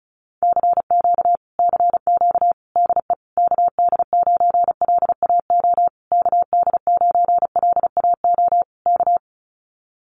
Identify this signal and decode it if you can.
Morse code, heard as a keyed tone, about 34 wpm, 700 Hz. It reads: CQ CQ DE KD9LAO KD9LAO K
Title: Common QSO Phrases Encore 35wpm